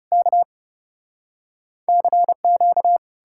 Morse code, heard as a keyed tone, about 40 wpm, 700 Hz. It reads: K CQ